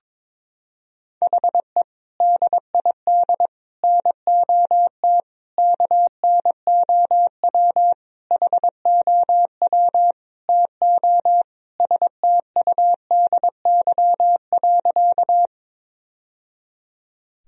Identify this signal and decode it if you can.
Morse code, heard as a keyed tone, about 22 wpm, 700 Hz. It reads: HE DID NOT KNOW HOW TO STUDY.
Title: Sentences from Top 1000 Words - Review Part 1 - 22wpm